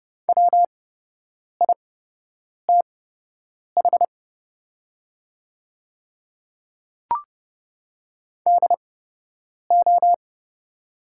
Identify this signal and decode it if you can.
Morse code, heard as a keyed tone, about 29 wpm, spaced as for 8 wpm, 690 Hz. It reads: WITH DO
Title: 8x Character Spacing - Top 300 Words - 30wpm